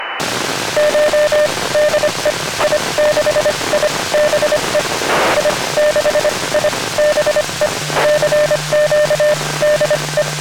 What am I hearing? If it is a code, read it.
Morse code, heard as a keyed tone, about 25 wpm, spaced as for 20 wpm, 600 Hz, with radio noise. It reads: CHDE I6IBE I6IBE CQDE
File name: i6ibe / xw2 / lilacsat2